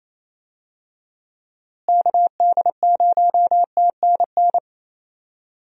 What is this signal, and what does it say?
Morse code, heard as a keyed tone, about 28 wpm, 700 Hz. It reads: KD0TNN
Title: USA Call Signs - 6 Characters - Repeated 3x - 28wpm